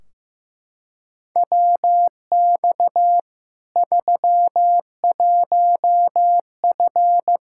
Morse code, heard as a keyed tone, about 16 wpm, 700 Hz. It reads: WX 31F
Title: Practice QSO - Exchange 2 - Info - 15wpm